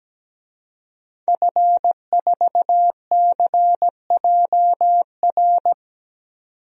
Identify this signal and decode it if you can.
Morse code, heard as a keyed tone, about 17 wpm, 700 Hz. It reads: F4CJR
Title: DXCC-100 Call Signs - 5 Characters - 17wpm